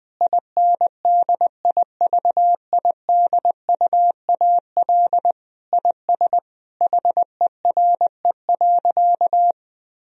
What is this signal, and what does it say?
Morse code, heard as a keyed tone, about 20 wpm, 700 Hz. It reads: INDIVIDUAL IS HERE.
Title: Sentences from Top 1000 Words - Review Part 4 - 20wpm